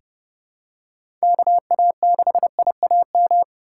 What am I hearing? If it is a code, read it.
Morse code, heard as a keyed tone, about 29 wpm, 700 Hz. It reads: KA6IAM
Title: USA Call Signs - 2x Character Spacing - 30wpm